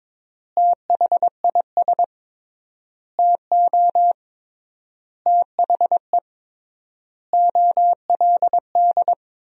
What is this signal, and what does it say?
Morse code, heard as a keyed tone, about 22 wpm, 700 Hz. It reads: THIS TO THE OLD
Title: 3x Word-Spacing - Sentences from Top 200 Words - 22wpm